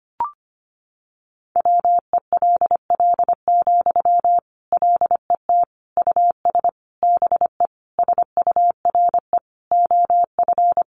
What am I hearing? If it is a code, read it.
Morse code, heard as a keyed tone, about 25 wpm, 700 Hz. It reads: WELL, LET US BE SURE OF
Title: Sentences from Top 300 Words 25wpm